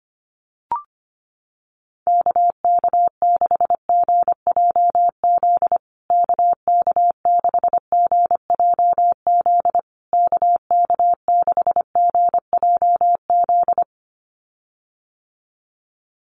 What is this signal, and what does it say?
Morse code, heard as a keyed tone, about 25 wpm, 700 Hz. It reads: KK6GJZ KK6GJZ KK6GJZ
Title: USA Call Signs Repeated 3x 25wpm